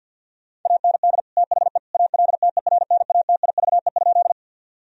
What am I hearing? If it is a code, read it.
Morse code, heard as a keyed tone, about 50 wpm, 700 Hz. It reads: AND THE ALTERNATIVE?
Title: Sentences from Top 1000 Words - Review Part 4 - 50wpm